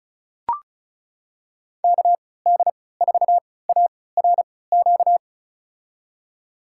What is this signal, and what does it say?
Morse code, heard as a keyed tone, about 36 wpm, spaced as for 20 wpm, 700 Hz. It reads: KD4ARQ
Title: USA Call Signs - 6 Characters - 3x Character Spacing - 35wpm